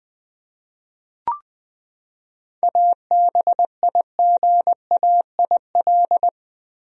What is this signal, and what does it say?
Morse code, heard as a keyed tone, about 20 wpm, 700 Hz. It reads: ABIGAIL
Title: USA Common Names - 20wpm